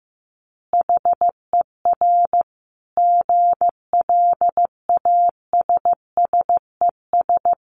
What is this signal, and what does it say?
Morse code, heard as a keyed tone, about 15 wpm, 700 Hz. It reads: HER GLASSES
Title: Additional Easy Phrases 15wpm